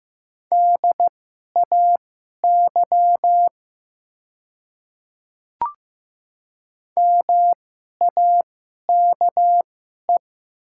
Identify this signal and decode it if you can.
Morse code, heard as a keyed tone, about 15 wpm, spaced as for 11 wpm, 700 Hz. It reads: DAY MAKE
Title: 2x Character Spacing - Top 100 Words - 15wpm